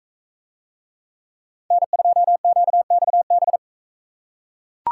Morse code, heard as a keyed tone, about 39 wpm, 700 Hz. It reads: N2QXB